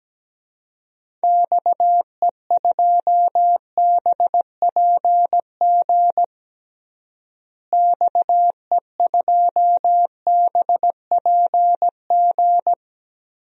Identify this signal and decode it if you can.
Morse code, heard as a keyed tone, about 17 wpm, 700 Hz. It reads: XE2BPG XE2BPG
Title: DXCC-100 Call Signs - 6 Characters - Repeated 3x - 17wpm